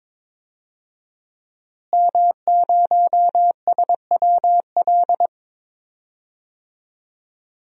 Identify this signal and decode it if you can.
Morse code, heard as a keyed tone, about 22 wpm, 700 Hz. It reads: M0SWL
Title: DXCC-100 Call Signs - 5 Characters - Repeated 3x - 22wpm